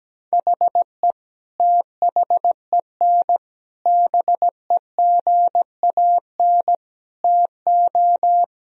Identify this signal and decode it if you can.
Morse code, heard as a keyed tone, about 17 wpm, 700 Hz. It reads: HE THEN BEGAN TO
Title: Sentences from Top 1000 Words - Review Part 3 - 17wpm